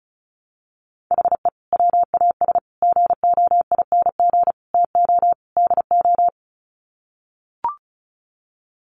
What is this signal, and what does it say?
Morse code, heard as a keyed tone, about 34 wpm, 700 Hz. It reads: HE WAS GOING TO DO
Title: Sets of 5 Words Encore2 35wpm